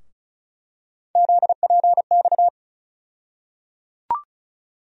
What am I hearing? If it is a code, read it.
Morse code, heard as a keyed tone, about 36 wpm, 700 Hz. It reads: ZPX
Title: USA Call Sign Postfixes - 3 Letters 35wpm